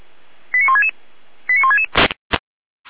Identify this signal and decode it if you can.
Five-tone selective call (ZVEI), heard as quick runs of five tones, tones of about 80 ms. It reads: 89166 89166